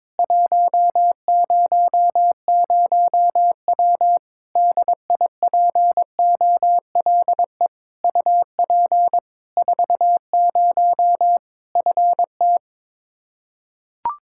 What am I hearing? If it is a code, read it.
Morse code, heard as a keyed tone, about 22 wpm, 700 Hz. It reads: 100W DIPOLE UP 40 FT
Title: Practice QSO - Exchange 2 - Info - 22wpm